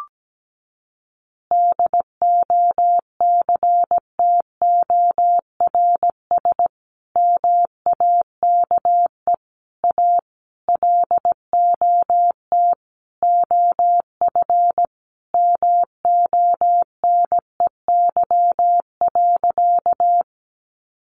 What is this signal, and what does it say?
Morse code, heard as a keyed tone, about 17 wpm, 700 Hz. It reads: DOCTORS MAKE A LOT OF MONEY.
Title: Easy Phrases 17wpm